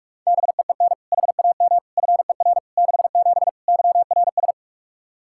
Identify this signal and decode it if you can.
Morse code, heard as a keyed tone, about 46 wpm, 700 Hz. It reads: BEEN HAM FER 67 YRS
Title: Practice QSO - Exchange 3 - More Info - 45wpm